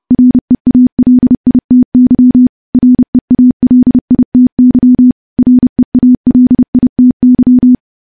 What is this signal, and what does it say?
Morse code, heard as a keyed tone, about 30 wpm, 250 Hz. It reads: REALITY REALITY REALITY
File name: morse / morse-training-30WPM